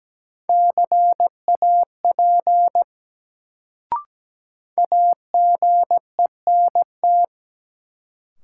Morse code, heard as a keyed tone, about 17 wpm, 700 Hz. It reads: CAP AGENT
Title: Sentences from Top 1600 Words - New Word Drill - 17wpm